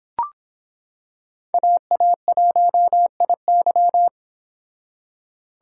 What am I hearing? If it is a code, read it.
Morse code, heard as a keyed tone, about 25 wpm, 700 Hz. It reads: AA1IY